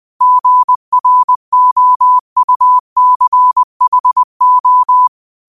Morse code, heard as a keyed tone, about 20 wpm, 1000 Hz. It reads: GROUCHO
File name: qotdfiles / 20wpm